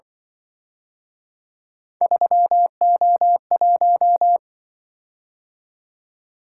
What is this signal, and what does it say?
Morse code, heard as a keyed tone, about 26 wpm, 700 Hz. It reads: 3O1